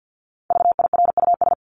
Morse code, heard as a keyed tone, about 50 wpm, 700 Hz. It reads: VIRUS
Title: Top 1251-1500 Words 50wpm